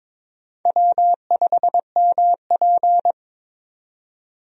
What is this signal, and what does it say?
Morse code, heard as a keyed tone, about 23 wpm, 700 Hz. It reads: W5MP
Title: USA Call Signs - 4 Characters - 22wpm